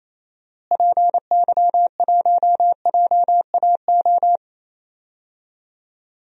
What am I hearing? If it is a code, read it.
Morse code, heard as a keyed tone, about 27 wpm, 700 Hz. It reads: PY1JAO
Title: DXCC-100 Call Signs - 6 Characters - Repeated 2x - 28wpm